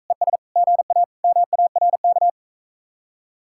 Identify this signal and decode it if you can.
Morse code, heard as a keyed tone, about 44 wpm, 700 Hz. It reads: ES GA MARK